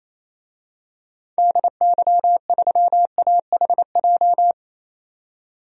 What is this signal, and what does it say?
Morse code, heard as a keyed tone, about 27 wpm, 700 Hz. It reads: DY3AHJ
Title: DXCC-100 Call Signs - 6 Characters - 28wpm